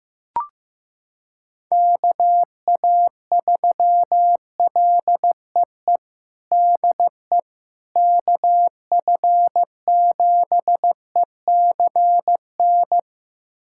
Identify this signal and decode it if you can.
Morse code, heard as a keyed tone, about 15 wpm, 700 Hz. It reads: KA3LEE DE KF7ECN
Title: USA Callsign DE Callsign - 6 Characters - 15wpm